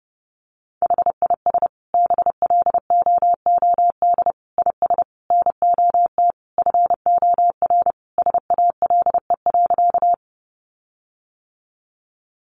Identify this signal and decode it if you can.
Morse code, heard as a keyed tone, about 30 wpm, 700 Hz. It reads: HIS BLOOD IS NOT FOR SALE.
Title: Sentences from Top 1400 Words 30wpm